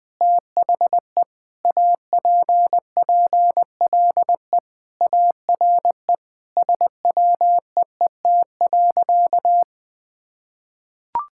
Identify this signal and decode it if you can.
Morse code, heard as a keyed tone, about 20 wpm, 700 Hz. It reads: THE APPLE ARE SWEET.
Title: Sentences with 3-Word Phrases - 20wpm